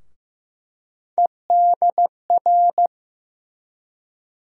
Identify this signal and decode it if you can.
Morse code, heard as a keyed tone, about 15 wpm, 700 Hz. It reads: EDR
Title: USA Call Sign Postfixes - 3 Letters 15wpm